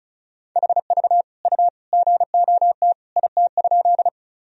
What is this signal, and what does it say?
Morse code, heard as a keyed tone, about 36 wpm, 700 Hz. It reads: HV U GOT IT?
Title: Sentences from Top 300 Words and Abbreviations 35wpm